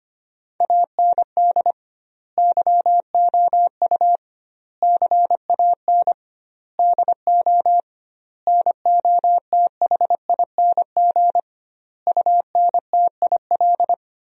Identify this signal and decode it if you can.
Morse code, heard as a keyed tone, about 25 wpm, 700 Hz. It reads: AND YOU CAN DO NOTHING UNTIL
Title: 2x Word-Spacing - Sentences from Top 500 Words - 25wpm